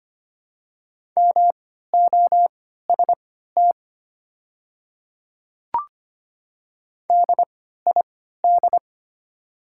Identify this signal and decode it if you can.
Morse code, heard as a keyed tone, about 25 wpm, spaced as for 14 wpm, 700 Hz. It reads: MOST DID
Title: 3x Character Spacing - Top 200 Words - 25wpm